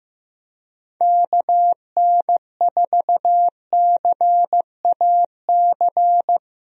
Morse code, heard as a keyed tone, about 15 wpm, 700 Hz. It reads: KN4CAC
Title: USA Call Signs - 6 Characters - Repeated 2x - 15wpm